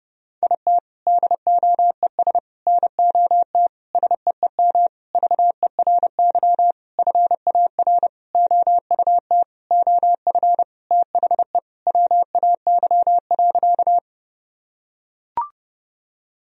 Morse code, heard as a keyed tone, about 30 wpm, 700 Hz. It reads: IT DOES NOT SEEM VERY FAR OUT OF THE WAY.
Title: Sentences from Top 1000 Words - Review Part 1 - 30wpm